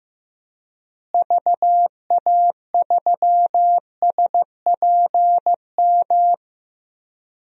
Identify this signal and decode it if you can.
Morse code, heard as a keyed tone, about 15 wpm, 700 Hz. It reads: VA3SPM
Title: DXCC-100 Call Signs - 6 Characters - Repeated 3x - 15wpm